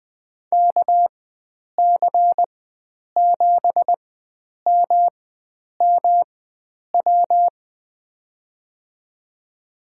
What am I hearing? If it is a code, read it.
Morse code, heard as a keyed tone, about 21 wpm, spaced as for 9 wpm, 700 Hz. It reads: KC7MMW
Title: USA Call Signs - 4x Character Spacing - 20wpm